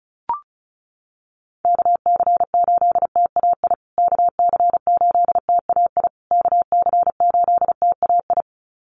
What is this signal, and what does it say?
Morse code, heard as a keyed tone, about 34 wpm, 700 Hz. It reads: KC8TAI KC8TAI KC8TAI
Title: USA Call Signs Repeated 3x 35wpm